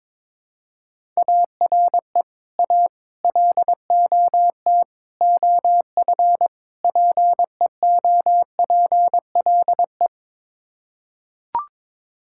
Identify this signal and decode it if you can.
Morse code, heard as a keyed tone, about 22 wpm, 700 Hz. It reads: ARE A LOT OF PEOPLE
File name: practice / 22wpm / Build A Phrase - 5-Word Phrases - 22wpm